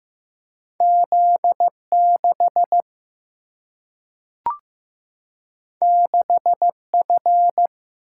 Morse code, heard as a keyed tone, about 15 wpm, 700 Hz. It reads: Z6 6F
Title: Single-Letter Single-Number 15wpm